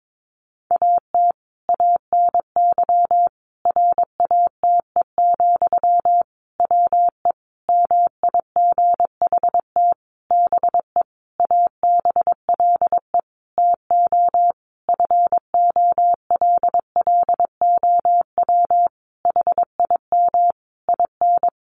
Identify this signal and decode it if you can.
Morse code, heard as a keyed tone, about 22 wpm, 700 Hz. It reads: AT ANY RATE, WE MIGHT BE ABLE TO FOLLOW HIM IN
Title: Sentences from Top 700 Words 22wpm